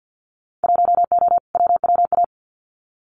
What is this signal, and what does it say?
Morse code, heard as a keyed tone, about 50 wpm, 700 Hz. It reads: 1O WWA
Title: Winter Field Day Exchanges 50wpm